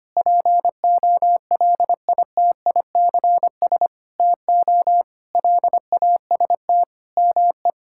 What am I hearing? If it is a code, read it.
Morse code, heard as a keyed tone, about 25 wpm, 700 Hz. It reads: POLITICS TO LAST ME
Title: Sentences from Top 1000 Words - Review Part 6 - 25wpm